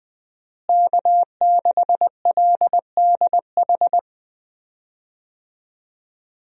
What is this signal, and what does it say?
Morse code, heard as a keyed tone, about 20 wpm, 700 Hz. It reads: K6LDH
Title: USA Call Signs - 5 Characters - Repeated 2x - 20wpm